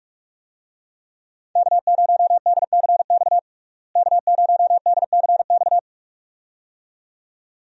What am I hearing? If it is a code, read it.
Morse code, heard as a keyed tone, about 41 wpm, 700 Hz. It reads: K0DCX K0DCX
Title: USA Call Signs - 5 Characters - Repeated 2x - 45wpm